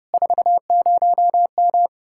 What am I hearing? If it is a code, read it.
Morse code, heard as a keyed tone, about 33 wpm, 700 Hz. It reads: 40M